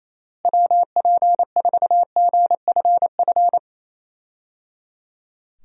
Morse code, heard as a keyed tone, about 28 wpm, 700 Hz. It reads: WP4GFF
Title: USA Call Signs - 6 Characters - Repeated 3x - 28wpm